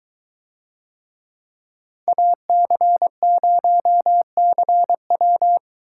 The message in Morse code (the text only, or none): AC0CW